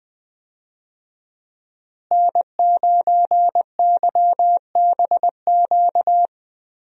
Morse code, heard as a keyed tone, about 20 wpm, 700 Hz. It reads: N9YBQ
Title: USA Call Signs - 5 Characters - Repeated 3x - 20wpm